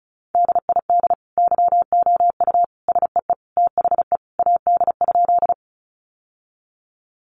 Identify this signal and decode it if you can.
Morse code, heard as a keyed tone, about 34 wpm, 700 Hz. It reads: DID YOU SEE THE AD?
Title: Sentences from Top 1300 Words 35wpm